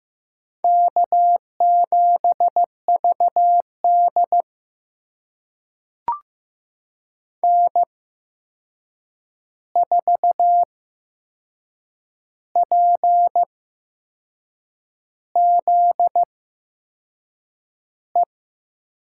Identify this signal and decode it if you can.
Morse code, heard as a keyed tone, about 15 wpm, spaced as for 6 wpm, 700 Hz. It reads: K7VD N4PZE
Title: USA Call Signs - 8x Character Spacing - 15wpm